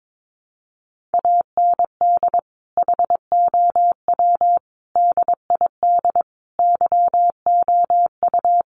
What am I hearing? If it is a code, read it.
Morse code, heard as a keyed tone, about 22 wpm, 700 Hz. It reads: AND HOW DID YOU
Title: Sentences from Top 1100 Words 22wpm